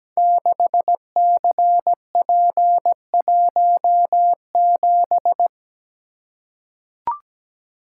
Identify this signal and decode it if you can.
Morse code, heard as a keyed tone, about 17 wpm, 700 Hz. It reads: 6CP17
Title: Code Groups - 5 Letter:Numbers - 17wpm